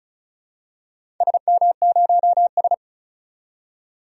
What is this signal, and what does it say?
Morse code, heard as a keyed tone, about 36 wpm, 690 Hz. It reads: SM0S
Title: DXCC-100 Call Signs - 4 Characters - 35wpm